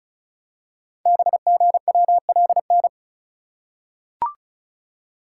Morse code, heard as a keyed tone, about 36 wpm, 700 Hz. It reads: BGWLN
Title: Code Groups - 5 Letters - 35wpm